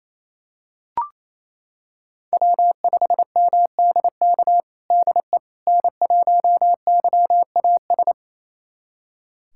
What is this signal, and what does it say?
Morse code, heard as a keyed tone, about 28 wpm, 700 Hz. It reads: W5MDK DE N1YAS
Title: USA Callsign DE Callsign - 5 Characters - 28wpm